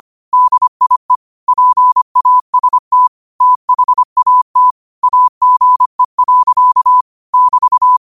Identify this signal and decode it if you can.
Morse code, heard as a keyed tone, about 25 wpm, 1000 Hz. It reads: DIE PAST THAT AGE. <BT>
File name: qotdfiles / 25wpm